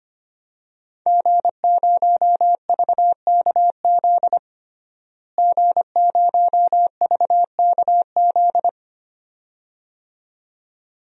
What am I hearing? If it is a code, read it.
Morse code, heard as a keyed tone, about 25 wpm, 700 Hz. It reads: G0VKZ G0VKZ
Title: DXCC-100 Call Signs - 5 Characters - Repeated 2x - 25wpm